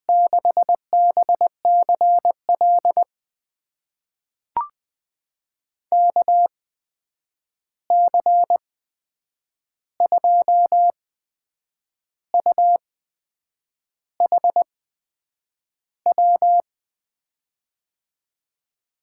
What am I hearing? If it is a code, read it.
Morse code, heard as a keyed tone, about 21 wpm, spaced as for 8 wpm, 700 Hz. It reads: 6BCL KC2UHW
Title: USA Call Signs - 6 Characters - 8x Character Spacing - 20wpm